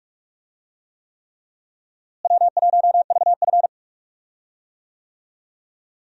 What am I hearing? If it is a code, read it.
Morse code, heard as a keyed tone, about 46 wpm, 700 Hz. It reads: W1VF
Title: USA Call Signs - 4 Characters - Repeated 3x - 45wpm